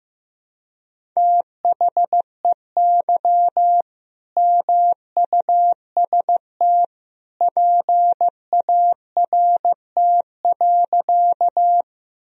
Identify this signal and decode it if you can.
Morse code, heard as a keyed tone, about 15 wpm, 700 Hz. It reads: THEY MUST PART.
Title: Sentences from Top 1000 Words - Review Part 1 - 15wpm